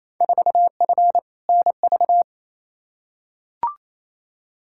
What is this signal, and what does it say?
Morse code, heard as a keyed tone, about 28 wpm, 700 Hz. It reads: 4F NV
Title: Field Day Exchanges Encore - 28wpm